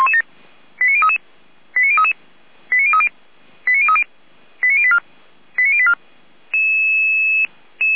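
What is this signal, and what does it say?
Five-tone selective call (ZVEI), heard as quick runs of five tones, tones of about 70 ms. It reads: 89033 89033 89030 89030 89084 89084
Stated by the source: ZVEI five-tone call 89503 received by SDR.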